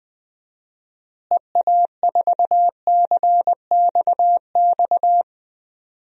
Morse code, heard as a keyed tone, about 19 wpm, 700 Hz. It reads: EA4CXX
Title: DXCC-100 Call Signs - 6 Characters - Repeated 3x - 20wpm